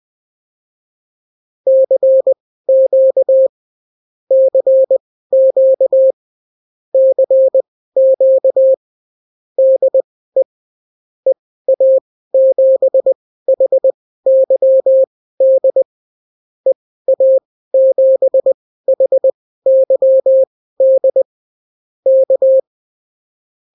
Morse code, heard as a keyed tone, about 20 wpm, spaced as for 14 wpm, 530 Hz. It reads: CQ CQ CQ DE EA7HYD EA7HYD K
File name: CW QSO libroCW4 20-10ppm